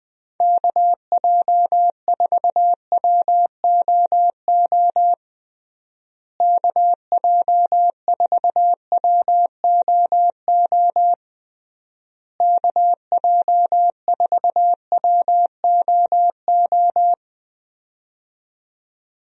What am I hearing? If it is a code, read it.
Morse code, heard as a keyed tone, about 20 wpm, 700 Hz. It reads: KJ4WOO KJ4WOO KJ4WOO
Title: USA Call Signs - 6 Characters - Repeated 3x - 20wpm